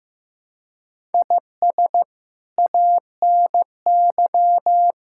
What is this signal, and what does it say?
Morse code, heard as a keyed tone, about 15 wpm, 700 Hz. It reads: IS ANY
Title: Sentences from Top 1000 Words - Review Part 6 - 15wpm